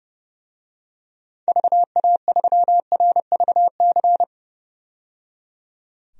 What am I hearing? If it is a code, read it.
Morse code, heard as a keyed tone, about 30 wpm, 700 Hz. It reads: VA3RVC
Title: DXCC-100 Call Signs - 6 Characters - 30wpm